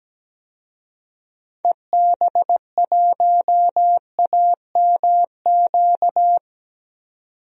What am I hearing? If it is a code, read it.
Morse code, heard as a keyed tone, about 17 wpm, 700 Hz. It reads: EB1AMQ
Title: DXCC-100 Call Signs - 6 Characters - 17wpm